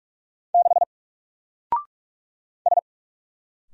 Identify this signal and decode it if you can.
Morse code, heard as a keyed tone, about 44 wpm, 700 Hz. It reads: 6 S
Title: Single Letter-Number Alt 45wpm